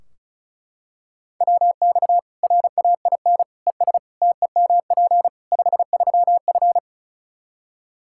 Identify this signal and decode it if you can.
Morse code, heard as a keyed tone, about 36 wpm, 700 Hz. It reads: WX RAIN ES TEMP 53F
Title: Practice QSO - Exchange 2 - Info - 35wpm